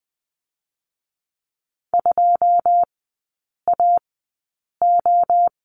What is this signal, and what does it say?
Morse code, heard as a keyed tone, about 20 wpm, spaced as for 8 wpm, 690 Hz. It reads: 2AO
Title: 081 Lesson 14e - Number 2 - Three Characters 20wpm